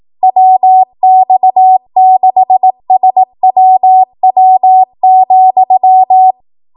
Morse code, heard as a keyed tone, about 18 wpm, 750 Hz. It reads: WX6SWW,
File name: Morse / 18WPM